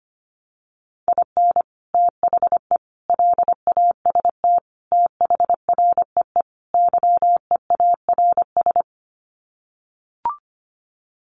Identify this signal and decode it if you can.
Morse code, heard as a keyed tone, about 25 wpm, 700 Hz. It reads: IN THE LAST THREE YEARS
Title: Sets of 5 Words Encore2 25wpm